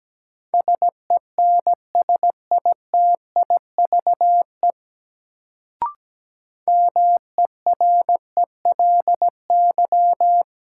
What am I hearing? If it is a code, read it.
Morse code, heard as a keyed tone, about 17 wpm, 700 Hz. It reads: SENSITIVE MERELY
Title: Sentences from Top 1400 Words - New Word Drill - 17wpm